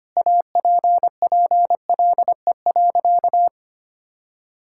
Morse code, heard as a keyed tone, about 25 wpm, 700 Hz. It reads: APPLE.
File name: practice / 25wpm / Sentences with 2-Word Phrases - 25wpm